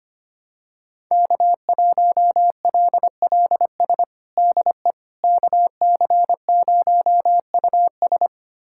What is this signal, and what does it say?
Morse code, heard as a keyed tone, about 25 wpm, 700 Hz. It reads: K1LLS DE KC0US
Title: USA Callsign DE Callsign - 5 Characters - 25wpm